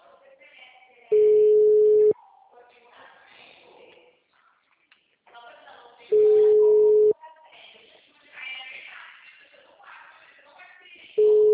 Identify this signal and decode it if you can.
Ringback tone (Europe, 425 Hz), with speech.
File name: out- -2014- - - .143